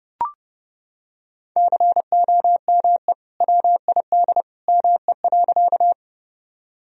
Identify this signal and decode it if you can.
Morse code, heard as a keyed tone, about 30 wpm, 700 Hz. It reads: COME WID ME.
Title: Sentences from Top 200 Words and Abbreviations 30wpm